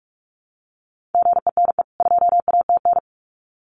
Morse code, heard as a keyed tone, about 44 wpm, 700 Hz. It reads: GENE 2ATN
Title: CWT Exchange 45wpm